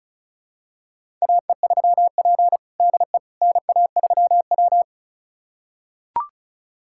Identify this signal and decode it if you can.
Morse code, heard as a keyed tone, about 34 wpm, 700 Hz. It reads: AE3P DE NA3W
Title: USA Callsign DE Callsign - 4 Characters - 35wpm